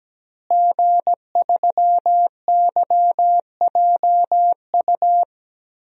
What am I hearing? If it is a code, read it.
Morse code, heard as a keyed tone, about 17 wpm, 700 Hz. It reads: G3YJU
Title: DXCC-100 Call Signs - 5 Characters - Repeated 3x - 17wpm